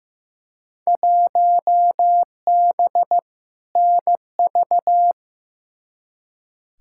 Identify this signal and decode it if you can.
Morse code, heard as a keyed tone, about 15 wpm, 700 Hz. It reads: 1B NV
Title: Field Day Exchanges - 15wpm